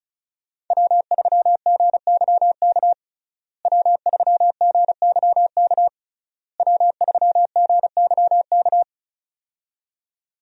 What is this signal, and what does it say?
Morse code, heard as a keyed tone, about 36 wpm, 700 Hz. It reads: W3GYK W3GYK W3GYK
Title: USA Call Signs - 5 Characters - Repeated 3x - 35wpm